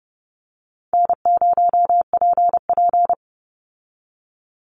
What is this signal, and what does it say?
Morse code, heard as a keyed tone, about 30 wpm, 700 Hz. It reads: N0PP
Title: Easy Call Signs - 30wpm